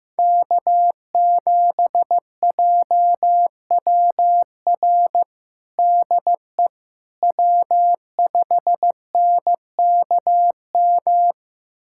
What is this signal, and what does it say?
Morse code, heard as a keyed tone, about 15 wpm, 700 Hz. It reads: K7JWR DE W5NKM
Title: USA Callsign DE Callsign 15wpm